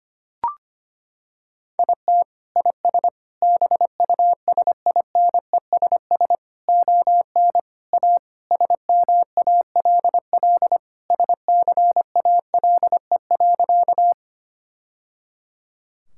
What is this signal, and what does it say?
Morse code, heard as a keyed tone, about 25 wpm, 700 Hz. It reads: IT IS BUSINESS ON A SMALL SCALE.
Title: Sentences from Top 1000 Words - Review Part 5 - 25wpm